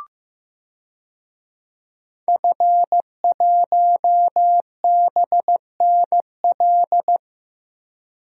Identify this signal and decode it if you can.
Morse code, heard as a keyed tone, about 15 wpm, 700 Hz. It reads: F1BNL
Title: DXCC-100 Call Signs - 5 Characters - Repeated 2x - 15wpm